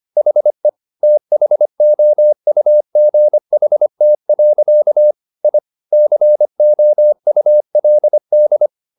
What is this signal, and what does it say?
Morse code, heard as a keyed tone, about 25 wpm, 600 Hz. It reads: HE THOUGHT. I COULD